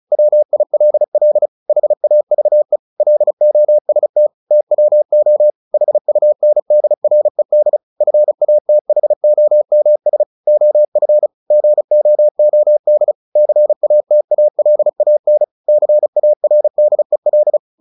Morse code, heard as a keyed tone, about 34 wpm, 600 Hz. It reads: WILL HAVE LOST TWO HUNDRED FATHOMS OF GOOD CATALAN CARDEL
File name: the old man and the sea / the old man and the sea 1 35wpm